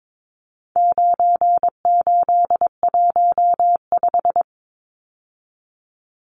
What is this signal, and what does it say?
Morse code, heard as a keyed tone, about 22 wpm, 700 Hz. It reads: 9815
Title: Numbers - 4-Digit 22wpm